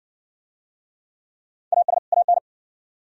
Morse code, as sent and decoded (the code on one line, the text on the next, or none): .- ... / .- -.
AS AN